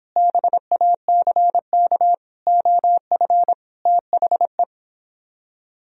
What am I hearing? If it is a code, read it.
Morse code, heard as a keyed tone, about 29 wpm, 700 Hz. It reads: BACK OF THE